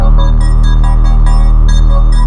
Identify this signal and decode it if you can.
There is no code.